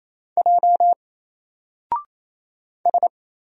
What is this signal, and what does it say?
Morse code, heard as a keyed tone, about 27 wpm, 700 Hz. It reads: J S
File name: Single-Letter-Number-28wpm